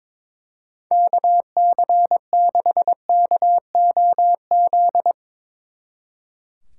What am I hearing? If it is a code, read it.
Morse code, heard as a keyed tone, about 24 wpm, 700 Hz. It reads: KC6KOZ
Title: Speed Racing - USA Call Signs 22wpm